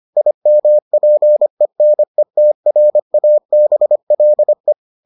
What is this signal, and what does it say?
Morse code, heard as a keyed tone, about 25 wpm, 600 Hz. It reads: IMPENETRABLE